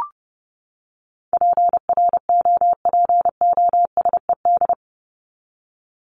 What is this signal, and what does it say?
Morse code, heard as a keyed tone, about 30 wpm, 700 Hz. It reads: PROPOSED
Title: ED Suffix - Top 100 Words with Base Words 30wpm